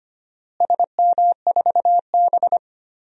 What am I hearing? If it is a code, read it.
Morse code, heard as a keyed tone, about 25 wpm, 700 Hz. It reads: SM4B
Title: DXCC-100 Call Signs - 4 Characters - Repeated 3x - 25wpm